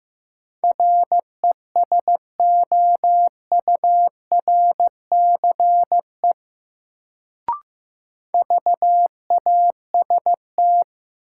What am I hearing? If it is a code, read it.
Morse code, heard as a keyed tone, about 15 wpm, 700 Hz. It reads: RESOURCE VAST
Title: Sentences from Top 1300 Words - New Word Drill Alt - 15wpm